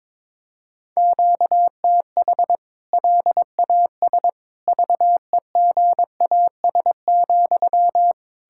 Morse code, heard as a keyed tone, about 22 wpm, 700 Hz. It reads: QTH LAS VEGAS,
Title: K1USN SST QSO Practice 22wpm